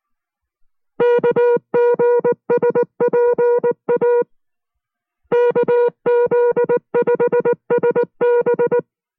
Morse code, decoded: KGSPA KZ5SB